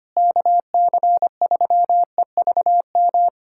Morse code, heard as a keyed tone, about 25 wpm, 700 Hz. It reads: KC3EVM
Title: USA Callsign DE Callsign 25wpm